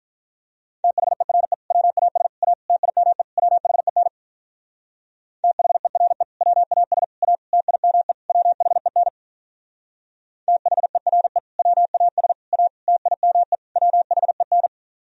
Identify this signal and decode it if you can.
Morse code, heard as a keyed tone, about 44 wpm, 700 Hz. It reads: THERE WAS A TIME WHEN THERE WAS A TIME WHEN THERE WAS A TIME WHEN